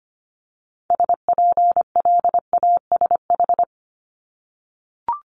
Morse code, heard as a keyed tone, about 25 wpm, 700 Hz. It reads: SPLASH
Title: Multi-letter Phonemes 25wpm